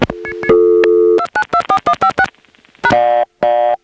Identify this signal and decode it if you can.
DTMF, heard as telephone touch tones, tones of about 80 ms. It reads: 2924156 #
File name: phonetrips / SmyrnaGA5HQ